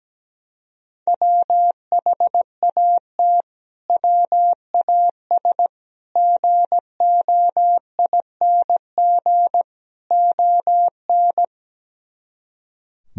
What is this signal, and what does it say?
Morse code, heard as a keyed tone, about 17 wpm, 700 Hz. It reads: WHAT WAS GOING ON
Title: Speed-Racing - Sets of 4 Words 17wpm